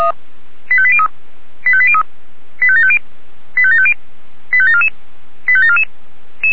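Five-tone selective call (ZVEI), heard as quick runs of five tones, tones of about 70 ms. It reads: 85803 85803 85750 85750 85744 85744
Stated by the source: ZVEI five-tone selective call 85730 received by SDR.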